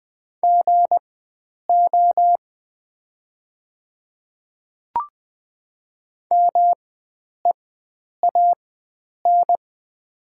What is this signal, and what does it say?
Morse code, heard as a keyed tone, about 20 wpm, spaced as for 9 wpm, 690 Hz. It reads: GO MEAN
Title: 4x Character Spacing - Top 300 Words - 20wpm